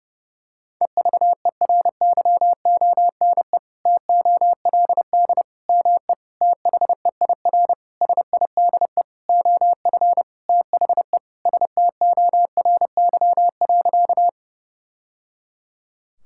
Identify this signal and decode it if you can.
Morse code, heard as a keyed tone, about 30 wpm, 700 Hz. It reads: EVERYONE TOLD ME THEIR SIDE OF THE STORY.
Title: Sentences from Top 1000 Words - Review Part 5 - 30wpm